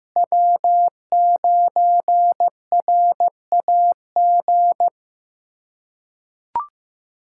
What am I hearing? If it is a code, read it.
Morse code, heard as a keyed tone, about 15 wpm, 700 Hz. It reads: W9RAG
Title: USA Callsign DE Callsign - 5 Characters - 15wpm